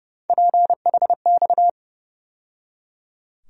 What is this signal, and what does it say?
Morse code, heard as a keyed tone, about 30 wpm, 700 Hz. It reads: PHX
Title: USA Call Sign Postfixes - 3 Letters 30wpm